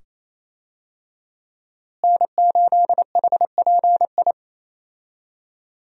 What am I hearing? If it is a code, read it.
Morse code, heard as a keyed tone, about 28 wpm, 700 Hz. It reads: N8HPI
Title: USA Call Signs - 5 Characters - Repeated 2x - 28wpm